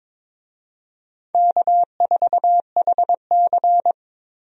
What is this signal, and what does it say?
Morse code, heard as a keyed tone, about 22 wpm, 700 Hz. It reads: K4HC
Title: USA Call Signs - 4 Characters - Repeated 2x - 22wpm